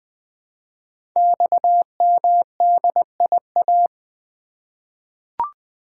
Morse code, heard as a keyed tone, about 20 wpm, 700 Hz. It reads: XMDIA
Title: Code Groups - 5 Letters - 20wpm